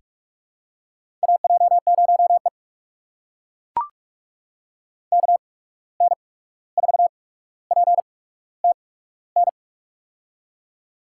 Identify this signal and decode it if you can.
Morse code, heard as a keyed tone, about 43 wpm, 690 Hz. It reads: A J 0 E K N 4 P T N
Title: USA Call Signs - 8x Character Spacing - 45wpm